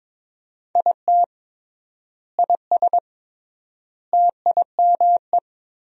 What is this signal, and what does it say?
Morse code, heard as a keyed tone, about 21 wpm, 700 Hz. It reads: IT IS TIME
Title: 3x Word-Spacing - Sentences from Top 500 Words - 22wpm